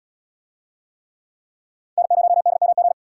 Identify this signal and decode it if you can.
Morse code, heard as a keyed tone, about 77 wpm, 700 Hz. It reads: N0DMG